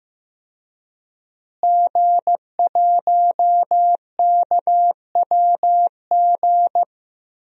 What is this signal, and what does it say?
Morse code, heard as a keyed tone, about 15 wpm, 700 Hz. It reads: G1KWG
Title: DXCC-100 Call Signs - 5 Characters - Repeated 2x - 15wpm